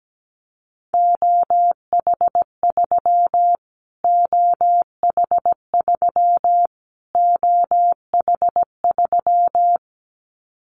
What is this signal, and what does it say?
Morse code, heard as a keyed tone, about 17 wpm, 700 Hz. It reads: OH3 OH3 OH3
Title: Callsign Prefixes for Top 25 DXCC - Repeated 3x - 17wpm